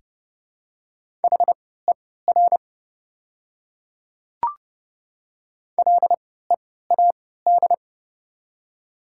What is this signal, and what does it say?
Morse code, heard as a keyed tone, about 30 wpm, spaced as for 17 wpm, 690 Hz. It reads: HER LEAD
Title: 3x Character Spacing - Top 500 Words - 30wpm